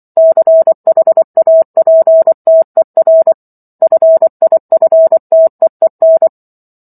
Morse code, heard as a keyed tone, about 24 wpm, 650 Hz. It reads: CHAPTER FIFTEEN